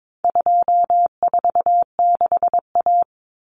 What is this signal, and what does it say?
Morse code, heard as a keyed tone, about 22 wpm, 700 Hz. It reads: 246A
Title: Common Cut Numbers - 4-Digit 22wpm